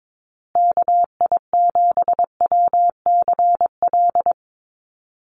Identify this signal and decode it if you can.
Morse code, heard as a keyed tone, about 22 wpm, 700 Hz. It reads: KI7WCL
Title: USA Call Signs Repeated 3x 22wpm